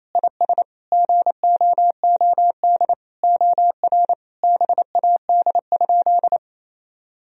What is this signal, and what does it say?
Morse code, heard as a keyed tone, about 28 wpm, 700 Hz. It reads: IS GOOD OR BAD?
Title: Sentences from Top 1000 Words - Review Part 1 - 28wpm